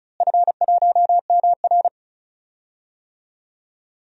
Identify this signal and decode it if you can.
Morse code, heard as a keyed tone, about 36 wpm, 700 Hz. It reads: F1MR
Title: DXCC-100 Call Signs - 4 Characters - Repeated 2x - 35wpm